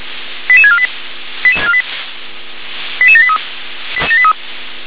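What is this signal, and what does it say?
Five-tone selective call (ZVEI), heard as quick runs of five tones, tones of about 70 ms. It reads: 88648 88783
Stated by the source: ZVEI five-tone call received on SDR.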